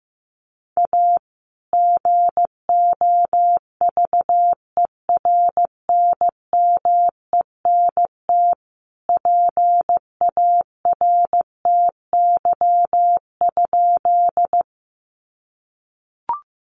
Morse code, heard as a keyed tone, about 15 wpm, 700 Hz. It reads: A GOVERNMENT PARTY?
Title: Sentences from Top 1400 Words 15wpm